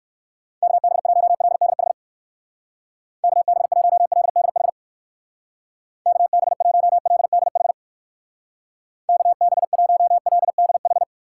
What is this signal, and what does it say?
Morse code, heard as a keyed tone, about 56 wpm, 700 Hz. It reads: KB1LDH KB1LDH KB1LDH KB1LDH
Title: Speed Racing - USA 2x3 Call Signs 45wpm